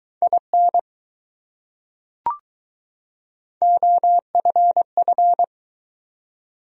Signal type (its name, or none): Morse code